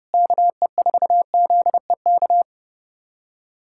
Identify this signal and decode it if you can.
Morse code, heard as a keyed tone, about 30 wpm, 700 Hz. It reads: KE4ZEK
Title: USA Callsign DE Callsign - 6 Characters - 30wpm